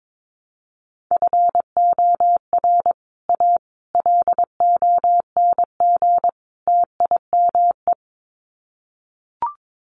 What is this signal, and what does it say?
Morse code, heard as a keyed tone, about 22 wpm, 700 Hz. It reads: FOR A LONG TIME
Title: Sets of 4 Words 22wpm